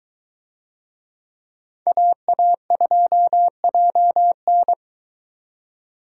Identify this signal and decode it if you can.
Morse code, heard as a keyed tone, about 22 wpm, 700 Hz. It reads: AA2JN